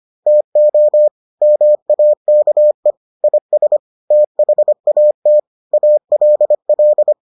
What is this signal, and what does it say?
Morse code, heard as a keyed tone, about 25 wpm, 600 Hz. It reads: TO MAKE IS THAT ALL